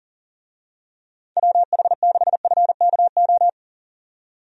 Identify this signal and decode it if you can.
Morse code, heard as a keyed tone, about 37 wpm, 700 Hz. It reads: WH6FKO
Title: USA Call Signs - 6 Characters - 40wpm